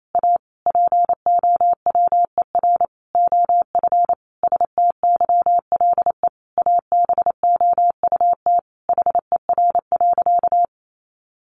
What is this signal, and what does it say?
Morse code, heard as a keyed tone, about 28 wpm, 700 Hz. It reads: A POWER OF STYLE ABOUT HER.
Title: Sentences from Top 800 Words 28wpm